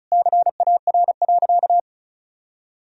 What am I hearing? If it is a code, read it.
Morse code, heard as a keyed tone, about 34 wpm, 700 Hz. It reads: CAR.
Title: Sentences from Top 1000 Words - Review Part 1 - 35wpm